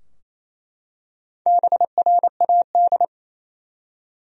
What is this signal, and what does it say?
Morse code, heard as a keyed tone, about 27 wpm, 700 Hz. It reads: BRAD
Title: USA Common Names - Max 4 Letters - 28wpm